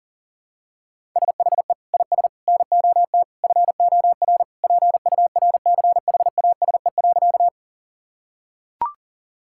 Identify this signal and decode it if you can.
Morse code, heard as a keyed tone, about 40 wpm, 700 Hz. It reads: SHE IS NOT FOR PURCHASE.
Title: Sentences from Top 1000 Words - Review Part 5 - 40wpm